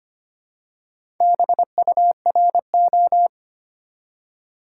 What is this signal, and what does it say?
Morse code, heard as a keyed tone, about 25 wpm, 700 Hz. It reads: BURO
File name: Top 90 QSO Elements - 4x Character Spacing - 25wpm